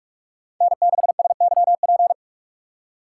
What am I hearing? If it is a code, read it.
Morse code, heard as a keyed tone, about 45 wpm, 700 Hz. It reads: N6SYP
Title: USA Call Signs - 5 Characters - 6x Character Spacing - 45wpm